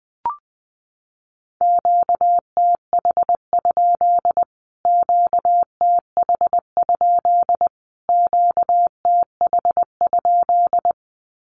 Morse code, heard as a keyed tone, about 20 wpm, 700 Hz. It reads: QTH? QTH? QTH?
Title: Q-Codes Repeated 3x 20wpm